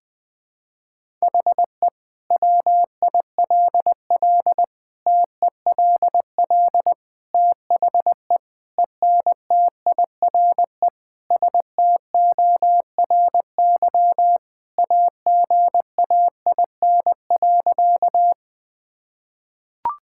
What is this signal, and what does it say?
Morse code, heard as a keyed tone, about 20 wpm, 700 Hz. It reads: HE WILL TELL THE ENTIRE STORY AGAIN.
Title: Sentences from Top 1000 Words - Review Part 5 - 20wpm